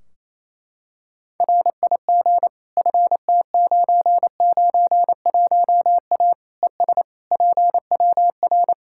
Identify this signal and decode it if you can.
Morse code, heard as a keyed tone, about 28 wpm, 700 Hz. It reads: RIG FT991A ES PWR
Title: Practice QSO - Exchange 2 - Info - 28wpm